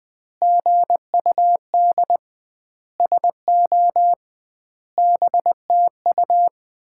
Morse code, heard as a keyed tone, about 21 wpm, 700 Hz. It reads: GUD SO BTU
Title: Common QSO Phrases - 2x Word-Spacing - 20wpm